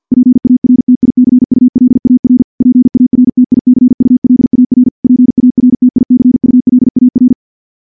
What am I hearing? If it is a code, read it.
Morse code, heard as a keyed tone, about 50 wpm, 250 Hz. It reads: PARTICULAR PARTICULAR PARTICULAR
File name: morse / morse-training-50WPM